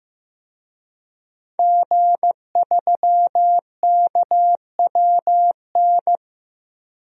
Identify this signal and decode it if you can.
Morse code, heard as a keyed tone, about 15 wpm, 700 Hz. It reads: G3KWN
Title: DXCC-100 Call Signs - 5 Characters - 15wpm